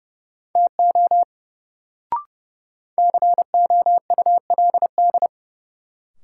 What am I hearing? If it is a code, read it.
Morse code, heard as a keyed tone, about 30 wpm, 700 Hz. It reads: TO COULD
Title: Sets of 2 Words - New Word Drill - 30wpm